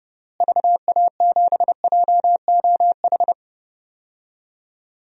Morse code, heard as a keyed tone, about 30 wpm, 700 Hz. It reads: VA7JOH
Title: DXCC-100 Call Signs - 6 Characters - Repeated 2x - 30wpm